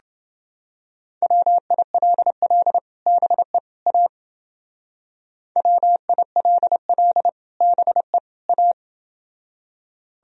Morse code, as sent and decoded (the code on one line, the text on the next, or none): .-- .. .-.. .-.. / -... / .- / .-- .. .-.. .-.. / -... / .-
WILL BE A WILL BE A